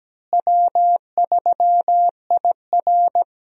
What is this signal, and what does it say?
Morse code, heard as a keyed tone, about 17 wpm, 700 Hz. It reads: W3IR
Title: USA Call Signs - 4 Characters - Repeated 3x - 17wpm